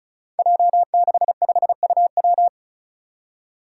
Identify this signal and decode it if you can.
Morse code, heard as a keyed tone, about 34 wpm, 700 Hz. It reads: J65UW
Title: Five Challenging Letter-Number Combinations 35wpm